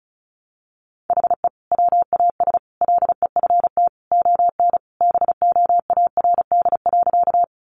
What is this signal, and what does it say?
Morse code, heard as a keyed tone, about 34 wpm, 700 Hz. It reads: HE WAS LEFT ON BOARD.
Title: Sentences from Top 700 Words 35wpm